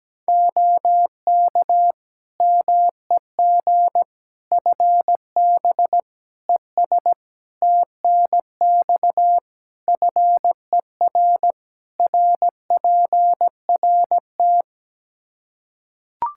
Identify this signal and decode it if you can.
Morse code, heard as a keyed tone, about 17 wpm, 700 Hz. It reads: OK MEG FB ES TNX FER RPRT
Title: Practice QSO - Exchange 2 - Info - 17wpm